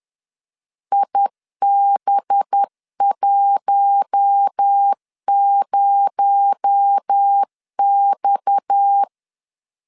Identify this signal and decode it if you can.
Morse code, heard as a keyed tone, about 10 wpm, 800 Hz. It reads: IB10X